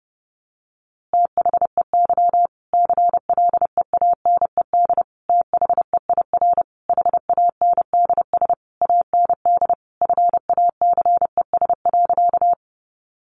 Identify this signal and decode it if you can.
Morse code, heard as a keyed tone, about 28 wpm, 700 Hz. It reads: THEY CLEANED THEIR HANDS AND FACES.
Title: A Sound Of Thunder 30wpm